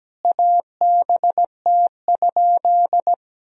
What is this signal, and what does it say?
Morse code, heard as a keyed tone, about 17 wpm, 700 Hz. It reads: ABT?
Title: Sentences from Top 100 Words and Abbreviations 17wpm